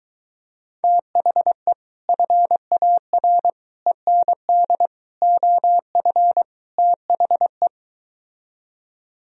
Morse code, heard as a keyed tone, about 23 wpm, 700 Hz. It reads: THE FAR END OF THE